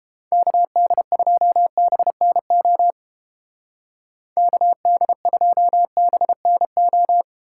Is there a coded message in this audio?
Yes: Morse code from 0.3 to 7.2 s, heard as a keyed tone, about 28 wpm, 700 Hz. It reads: KD2BNO KD2BNO